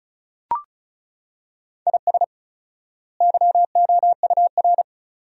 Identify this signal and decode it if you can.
Morse code, heard as a keyed tone, about 36 wpm, 700 Hz. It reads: IS YOUR
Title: 4x Word-Spacing - Sentences from Top 400 Words - 35wpm